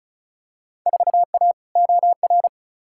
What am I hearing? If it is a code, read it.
Morse code, heard as a keyed tone, about 36 wpm, 700 Hz. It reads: 4A OR
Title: Field Day Exchanges Encore - 35wpm